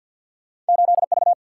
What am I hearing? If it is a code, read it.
Morse code, heard as a keyed tone, about 50 wpm, 700 Hz. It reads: 8 V
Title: Single-Letter Single-Number 50wpm